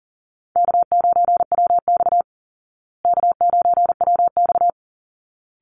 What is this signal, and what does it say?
Morse code, heard as a keyed tone, about 40 wpm, 700 Hz. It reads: K9WX K9WX
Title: Call Signs Repeated 3x - 40wpm